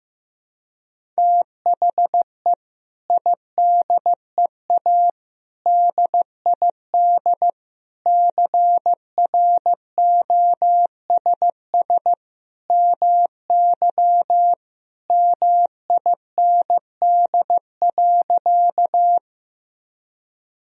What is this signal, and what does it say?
Morse code, heard as a keyed tone, about 15 wpm, 700 Hz. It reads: THE IDEA DID CROSS MY MIND.
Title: Sentences from Top 1000 Words - Review Part 1 - 15wpm